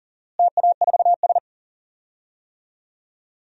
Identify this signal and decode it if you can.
Morse code, heard as a keyed tone, about 40 wpm, 700 Hz. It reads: TA4S
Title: DXCC-100 Call Signs - 4 Characters - Repeated 3x - 40wpm